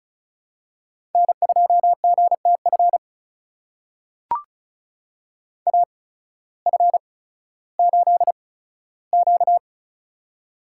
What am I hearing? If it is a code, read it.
Morse code, heard as a keyed tone, about 36 wpm, spaced as for 16 wpm, 700 Hz. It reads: N2GTF AF8Q
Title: USA Call Signs - 8x Character Spacing - 35wpm